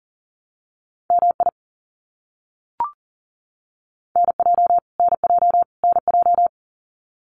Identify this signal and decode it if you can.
Morse code, heard as a keyed tone, about 40 wpm, 700 Hz. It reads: MI NJ NJ NJ
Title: US States Repeated 3x 40wpm